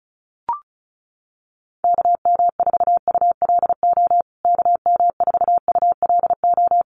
Morse code, heard as a keyed tone, about 34 wpm, 700 Hz. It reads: KM4ULO KM4ULO
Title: USA Call Signs Repeated 3x 35wpm